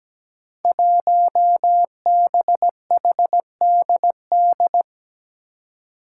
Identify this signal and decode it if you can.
Morse code, heard as a keyed tone, about 17 wpm, 700 Hz. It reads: 1BHDD
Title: Five Challenging Letter-Number Combinations 17wpm